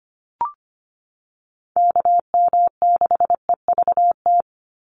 Morse code, heard as a keyed tone, about 25 wpm, 700 Hz. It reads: KM6EVT
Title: USA Call Signs 25wpm